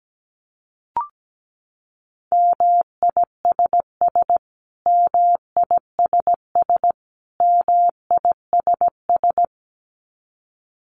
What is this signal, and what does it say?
Morse code, heard as a keyed tone, about 17 wpm, 700 Hz. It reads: MISS MISS MISS
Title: Top 500 Words Repeated 17wpm